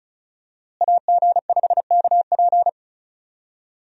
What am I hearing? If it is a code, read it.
Morse code, heard as a keyed tone, about 34 wpm, 700 Hz. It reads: AG5KP
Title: USA Call Signs - 5 Characters - Repeated 2x - 35wpm